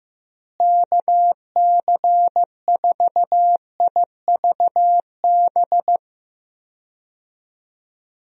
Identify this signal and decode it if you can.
Morse code, heard as a keyed tone, about 15 wpm, 700 Hz. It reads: KC4IVB
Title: USA Call Signs - 6 Characters - Repeated 3x - 15wpm